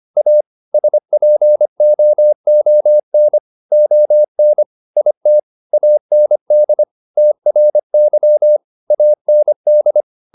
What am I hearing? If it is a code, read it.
Morse code, heard as a keyed tone, about 25 wpm, 600 Hz. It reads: A SPOON ON IT AND TRY AND